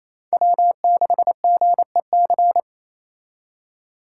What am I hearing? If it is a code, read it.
Morse code, heard as a keyed tone, about 35 wpm, 700 Hz. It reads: W6GEC